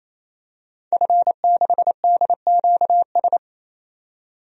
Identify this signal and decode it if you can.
Morse code, heard as a keyed tone, about 28 wpm, 700 Hz. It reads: F6DQS
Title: DXCC-100 Call Signs - 5 Characters - 28wpm